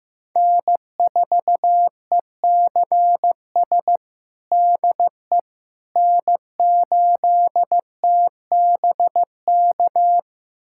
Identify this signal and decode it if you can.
Morse code, heard as a keyed tone, about 15 wpm, 700 Hz. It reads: N4ECS DE N8TBK
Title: USA Callsign DE Callsign - 5 Characters - 15wpm